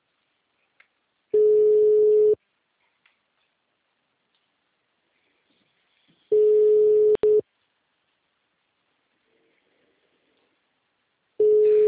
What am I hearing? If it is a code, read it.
Ringback tone (Europe, 425 Hz).
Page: Sao Carlos II